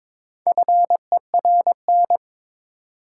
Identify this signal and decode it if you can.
Morse code, heard as a keyed tone, about 22 wpm, 700 Hz. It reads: FERN
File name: Difficult Words with Hints - 22wpm